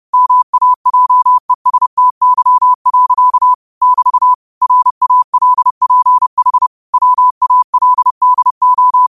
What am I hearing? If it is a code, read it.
Morse code, heard as a keyed tone, about 30 wpm, 1000 Hz. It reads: MAJESTY. <BT> RALPH WALDO